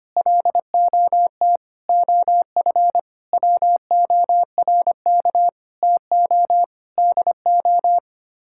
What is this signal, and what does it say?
Morse code, heard as a keyed tone, about 25 wpm, 700 Hz. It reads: LOT OF WORK TO DO
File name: Sets of 9 Words - 25wpm